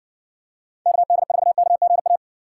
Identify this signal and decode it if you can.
Morse code, heard as a keyed tone, about 55 wpm, 700 Hz. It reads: KD4KGA